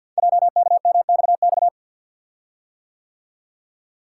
Morse code, heard as a keyed tone, about 48 wpm, 690 Hz. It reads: JKMXX
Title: Code Groups - 5 Letters - 50wpm